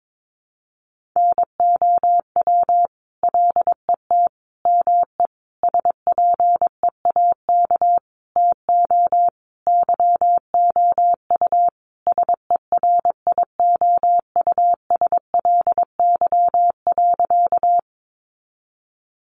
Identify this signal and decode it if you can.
Morse code, heard as a keyed tone, about 22 wpm, 700 Hz. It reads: NOW LET ME SPEAK TO YOU SERIOUSLY.
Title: Sentences from Top 1200 Words 22wpm